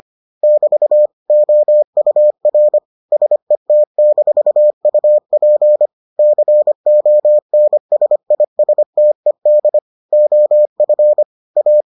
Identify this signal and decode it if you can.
Morse code, heard as a keyed tone, about 25 wpm, 600 Hz. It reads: <BT> OUR SET-UP CONSISTED OF A